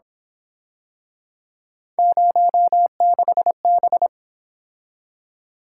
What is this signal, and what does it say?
Morse code, heard as a keyed tone, about 28 wpm, 700 Hz. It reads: 06B
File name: Speed-Racing - Three Letter-Number Combinations - 22wpm